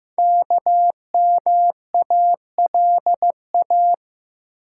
Morse code, heard as a keyed tone, about 15 wpm, 700 Hz. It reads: KMALA
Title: Code Groups - 5 Letters - 15wpm